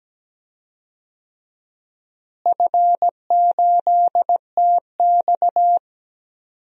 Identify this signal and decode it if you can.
Morse code, heard as a keyed tone, about 17 wpm, 700 Hz. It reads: F8TX
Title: DXCC-100 Call Signs - 4 Characters - Repeated 2x - 17wpm